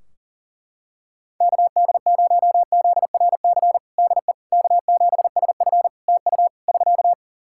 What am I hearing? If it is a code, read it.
Morse code, heard as a keyed tone, about 40 wpm, 700 Hz. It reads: KD0ZRC DE K7SF TU <SK>
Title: Practice QSO - Exchange 4 - Ending - 40wpm